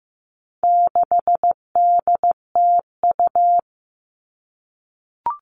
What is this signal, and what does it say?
Morse code, heard as a keyed tone, about 15 wpm, 700 Hz. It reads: 6DTU
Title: All Cut Numbers - 4-Digit 15wpm